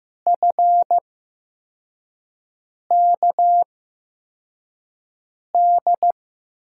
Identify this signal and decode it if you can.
Morse code, heard as a keyed tone, about 15 wpm, spaced as for 4 wpm, 700 Hz. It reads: FKD
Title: USA Call Signs - 6 Characters - 8x Character Spacing - 15wpm